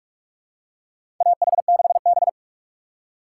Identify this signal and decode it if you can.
Morse code, heard as a keyed tone, about 46 wpm, 700 Hz. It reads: AH6B
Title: USA Call Signs - 4 Characters - 3x Character Spacing - 45wpm